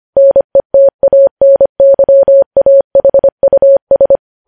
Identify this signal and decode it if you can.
Morse code, heard as a keyed tone, about 25 wpm, 550 Hz. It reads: NETANYAHUS
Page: news headlines in morse code at 25 wpm